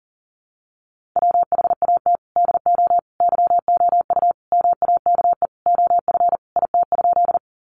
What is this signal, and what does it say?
Morse code, heard as a keyed tone, about 40 wpm, 700 Hz. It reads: WHAT DO YOU MAKE OF IT?
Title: Sentences from Top 100 Words 40wpm